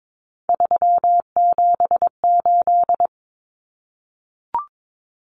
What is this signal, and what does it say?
Morse code, heard as a keyed tone, about 22 wpm, 700 Hz. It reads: 378
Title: Common Cut Numbers - 3-Digit 22wpm